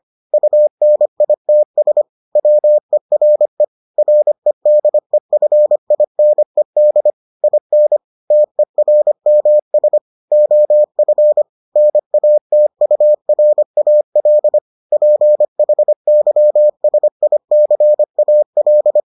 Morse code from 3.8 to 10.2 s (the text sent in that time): REDEFINED IN TERMS